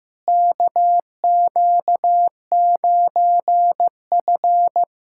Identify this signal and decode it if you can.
Morse code, heard as a keyed tone, about 15 wpm, 700 Hz. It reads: KQ9F
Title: USA Callsign DE Callsign - 4 Characters - 15wpm